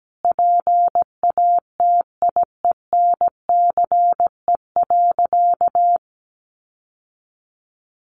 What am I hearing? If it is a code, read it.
Morse code, heard as a keyed tone, about 17 wpm, 700 Hz. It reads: PATIENCE.
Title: Sentences from Top 1300 Words 17wpm